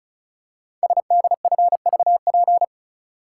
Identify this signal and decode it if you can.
Morse code, heard as a keyed tone, about 36 wpm, 700 Hz. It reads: SDFVP
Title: Code Groups - 5 Letter:Numbers - 35wpm